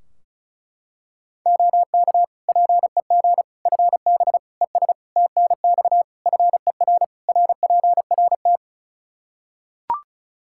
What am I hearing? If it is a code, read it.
Morse code, heard as a keyed tone, about 34 wpm, 700 Hz. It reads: OK PEG FB ES TNX FER RPRT
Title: Practice QSO - Exchange 2 - Info - 35wpm